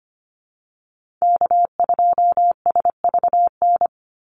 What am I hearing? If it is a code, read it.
Morse code, heard as a keyed tone, about 25 wpm, 700 Hz. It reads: K2SVN
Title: USA Call Signs 25wpm